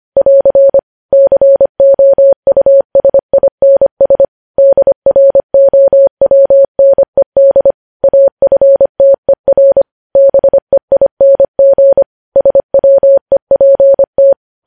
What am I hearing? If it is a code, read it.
Morse code, heard as a keyed tone, about 25 wpm, 550 Hz. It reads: <AR> COUSINS DROWNED AFTER BEING SWEPT
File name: podcast / News headlines-1x-25x25